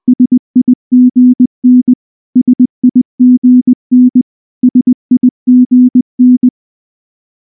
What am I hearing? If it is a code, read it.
Morse code, heard as a keyed tone, about 19 wpm, 250 Hz. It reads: SIGN SIGN SIGN
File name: morse-training-20WPM